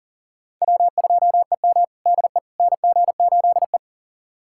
Morse code, heard as a keyed tone, about 40 wpm, 700 Hz. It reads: W2EM DE NG8E
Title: USA Callsign DE Callsign - 4 Characters - 40wpm